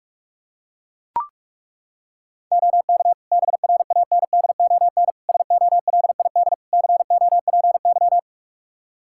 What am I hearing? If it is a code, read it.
Morse code, heard as a keyed tone, about 41 wpm, 700 Hz. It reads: OK BRANDON SOLID COPY
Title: K1USN SST QSO Practice 45wpm